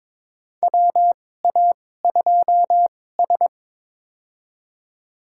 Morse code, heard as a keyed tone, about 22 wpm, spaced as for 16 wpm, 700 Hz. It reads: WA2S
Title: USA Call Signs - 4 Characters - 2x Character Spacing - 22wpm